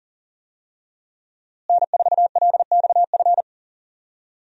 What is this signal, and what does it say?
Morse code, heard as a keyed tone, about 40 wpm, 700 Hz. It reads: N4LXF
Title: USA Call Signs - 5 Characters - Repeated 2x - 40wpm